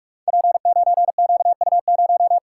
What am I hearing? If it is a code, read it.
Morse code, heard as a keyed tone, about 44 wpm, 700 Hz. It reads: P9QU0
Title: Code Groups - 5 Letter:Numbers - 45wpm